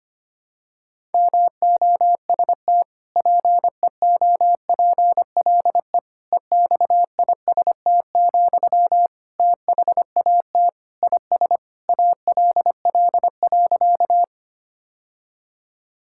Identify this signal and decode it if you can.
Morse code, heard as a keyed tone, about 25 wpm, 700 Hz. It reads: MOST PEOPLE EXIST, THAT IS ALL.
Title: Sentences from Top 1000 Words - Review Part 5 - 25wpm